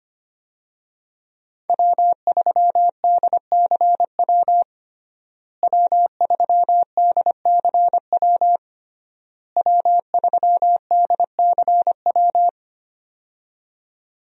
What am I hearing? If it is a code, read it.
Morse code, heard as a keyed tone, about 25 wpm, 700 Hz. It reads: W3DCW W3DCW W3DCW
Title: USA Call Signs - 5 Characters - Repeated 3x - 25wpm